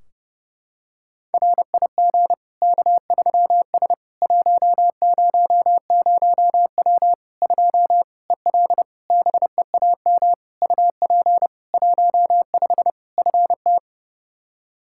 Morse code, heard as a keyed tone, about 32 wpm, 700 Hz. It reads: RIG K3S 100W 2 EL BEAM UP 15 FT
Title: Practice QSO - Exchange 2 - Info - 30wpm